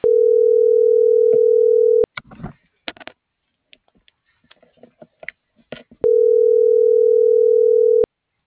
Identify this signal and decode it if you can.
Ringback tone (North America), 440 + 480 Hz.